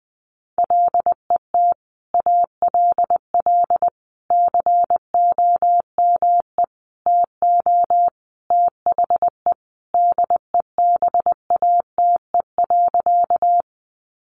Morse code, heard as a keyed tone, about 20 wpm, 700 Hz. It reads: LET ALL COME TO THE DEBATE.
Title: Sentences from Top 1100 Words 20wpm